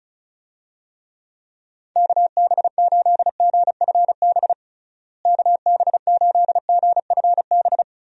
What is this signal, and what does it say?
Morse code, heard as a keyed tone, about 34 wpm, 700 Hz. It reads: KB8GFB KB8GFB
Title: USA Call Signs - 6 Characters - Repeated 2x - 35wpm